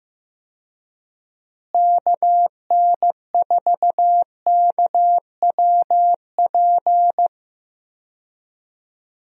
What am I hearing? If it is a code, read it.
Morse code, heard as a keyed tone, about 15 wpm, 700 Hz. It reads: KN4KWP
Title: USA Call Signs - 6 Characters - Repeated 2x - 15wpm